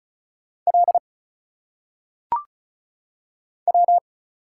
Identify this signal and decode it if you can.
Morse code, heard as a keyed tone, about 34 wpm, 700 Hz. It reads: L W
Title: Single Letters - Phonetic Pronunciation 35wpm